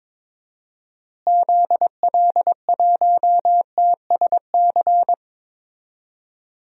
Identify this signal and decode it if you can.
Morse code, heard as a keyed tone, about 22 wpm, 700 Hz. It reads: ZL1TSC
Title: DXCC-100 Call Signs - 6 Characters - Repeated 3x - 22wpm